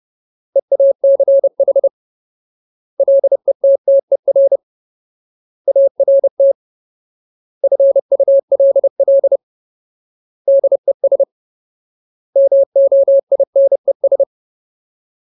Morse code, heard as a keyed tone, about 30 wpm, 550 Hz. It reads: EACH LETTER ART FULL DES MOINES
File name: Ultimate Word List / 30WPM Ultimate